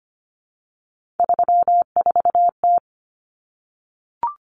Morse code, heard as a keyed tone, about 25 wpm, 700 Hz. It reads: 34T
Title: Common Cut Numbers - 3-Digit 25wpm